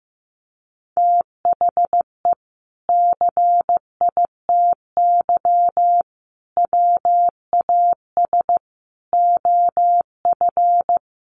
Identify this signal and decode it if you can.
Morse code, heard as a keyed tone, about 15 wpm, 700 Hz. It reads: THE CITY WAS OF
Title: Sentences from Top 1500 Words 15wpm